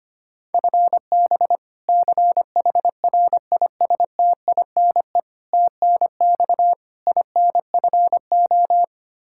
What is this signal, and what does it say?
Morse code, heard as a keyed tone, about 25 wpm, 700 Hz. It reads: FB CHRISTINE TNX INFO
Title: Practice QSO - Exchange 3 - More Info - 25wpm